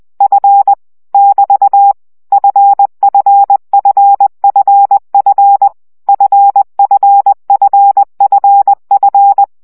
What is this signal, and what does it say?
Morse code, heard as a keyed tone, about 20 wpm, 800 Hz. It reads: F <BT> FFFFF FFFFF
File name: CW gyakorlas f 17 baud